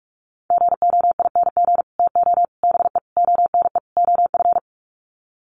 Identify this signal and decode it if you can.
Morse code, heard as a keyed tone, about 44 wpm, 700 Hz. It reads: GOING TO BE ONE OF
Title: Sets of 5 Words Encore2 45wpm